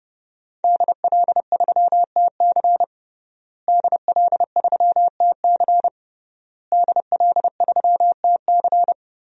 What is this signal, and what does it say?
Morse code, heard as a keyed tone, about 32 wpm, 700 Hz. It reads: DL3TC DL3TC DL3TC
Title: DXCC-100 Call Signs - 5 Characters - Repeated 3x - 30wpm